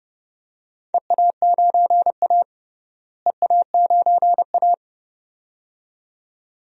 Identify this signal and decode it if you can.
Morse code, heard as a keyed tone, about 30 wpm, 700 Hz. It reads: EA9A EA9A
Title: DXCC-100 Call Signs - 4 Characters - Repeated 2x - 30wpm